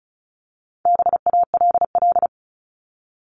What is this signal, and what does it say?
Morse code, heard as a keyed tone, about 34 wpm, 700 Hz. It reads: BALL
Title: Top 751-1000 Words 35wpm